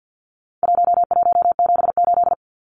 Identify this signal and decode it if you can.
Morse code, heard as a keyed tone, about 50 wpm, 700 Hz. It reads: 1178
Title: Numbers - 4-Digit 50wpm